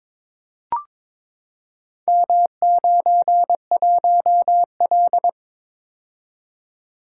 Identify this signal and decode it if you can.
Morse code, heard as a keyed tone, about 22 wpm, 700 Hz. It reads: M91L
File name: practice / 22wpm / Four Letter-Number Combinations - 22wpm